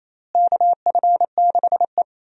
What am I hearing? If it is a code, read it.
Morse code, heard as a keyed tone, about 28 wpm, 700 Hz. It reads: KF6E
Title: USA Call Signs - 4 Characters - Repeated 2x - 28wpm